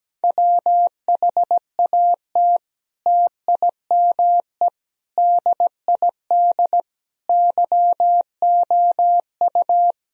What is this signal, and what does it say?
Morse code, heard as a keyed tone, about 17 wpm, 700 Hz. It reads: WHAT TIME DID YOU
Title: Sentences from Top 1000 Words - Review Part 1 - 17wpm